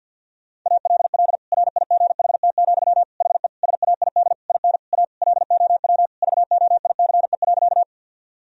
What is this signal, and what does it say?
Morse code, heard as a keyed tone, about 51 wpm, 700 Hz. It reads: ALL RIGHT, HE SAID IN A LOW VOICE.
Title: Sentences from Top 1000 Words - Review Part 1 - 50wpm